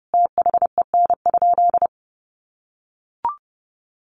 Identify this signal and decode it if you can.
Morse code, heard as a keyed tone, about 30 wpm, 700 Hz. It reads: THEN?
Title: Sentences from Top 1500 Words 30wpm